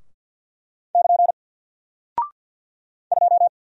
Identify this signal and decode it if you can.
Morse code, heard as a keyed tone, about 50 wpm, 700 Hz. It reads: <KN> 2
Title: Single Character (Letter:Number:Punctuation:Prosign) Alt 50wpm